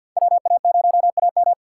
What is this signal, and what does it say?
Morse code, heard as a keyed tone, about 50 wpm, 700 Hz. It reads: WA0AM
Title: USA Call Signs - 5 Characters - 2x Character Spacing - 50wpm